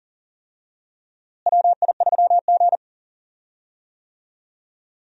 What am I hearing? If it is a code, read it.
Morse code, heard as a keyed tone, about 39 wpm, 700 Hz. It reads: WI3G
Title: USA Call Signs - 4 Characters - Repeated 3x - 40wpm